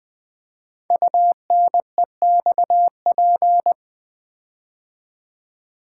Morse code, heard as a keyed tone, about 20 wpm, 700 Hz. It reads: UNEXP
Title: Build A Word - 5 Letters At A Time - 10-Letter Words - 20wpm